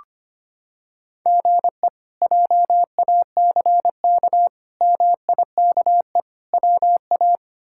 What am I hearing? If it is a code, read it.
Morse code, heard as a keyed tone, about 25 wpm, 700 Hz. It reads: GE JACK MIKE WA
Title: K1USN SST QSO Practice - 25wpm